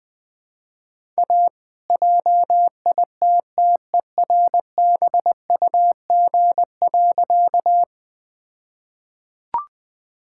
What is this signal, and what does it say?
Morse code, heard as a keyed tone, about 20 wpm, 700 Hz. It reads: A JITTERBUG.
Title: All-Age Appropriate Jokes - 20wpm